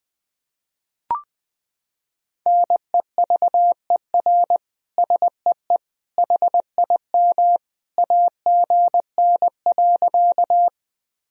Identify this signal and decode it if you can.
Morse code, heard as a keyed tone, about 20 wpm, 700 Hz. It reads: NEVER SEE HIM AGN.
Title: Sentences from Top 300 Words and Abbreviations 20wpm